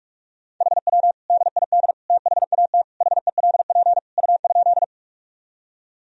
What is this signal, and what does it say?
Morse code, heard as a keyed tone, about 47 wpm, 700 Hz. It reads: HW DID THAT HELP U?
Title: Sentences from Top 200 Words and Abbreviations 45wpm